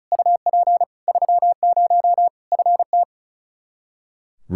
Morse code, heard as a keyed tone, about 36 wpm, 700 Hz. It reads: UP 30 FT
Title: Practice QSO - Exchange 2 - Info - 35wpm